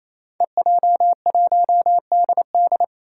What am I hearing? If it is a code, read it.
Morse code, heard as a keyed tone, about 28 wpm, 700 Hz. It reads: EJ1DD
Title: Five Challenging Letter-Number Combinations 28wpm